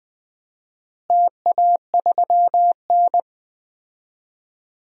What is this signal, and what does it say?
Morse code, heard as a keyed tone, about 20 wpm, 700 Hz. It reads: TA3N
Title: DXCC-100 Call Signs - 4 Characters - Repeated 2x - 20wpm